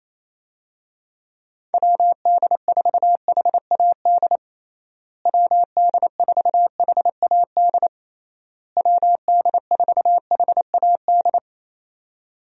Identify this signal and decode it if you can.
Morse code, heard as a keyed tone, about 28 wpm, 700 Hz. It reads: WD4HAD WD4HAD WD4HAD
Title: USA Call Signs - 6 Characters - Repeated 3x - 28wpm